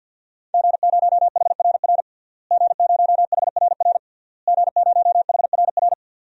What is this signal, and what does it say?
Morse code, heard as a keyed tone, about 52 wpm, 700 Hz. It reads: G0HRR G0HRR G0HRR
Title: DXCC-100 Call Signs - 5 Characters - Repeated 3x - 50wpm